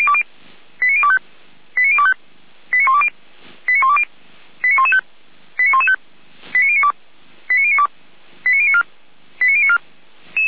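Five-tone selective call (ZVEI), heard as quick runs of five tones, tones of about 70 ms. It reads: 89025 89025 89120 89120 89115 89115 89092 89092 89094 89094
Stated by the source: ZVEI five-tone alert received by SDR.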